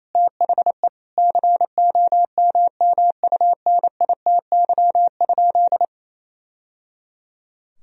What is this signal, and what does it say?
Morse code, heard as a keyed tone, about 28 wpm, 700 Hz. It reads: THE COMMUNITY?
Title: Sentences from Top 1000 Words - Review Part 4 - 28wpm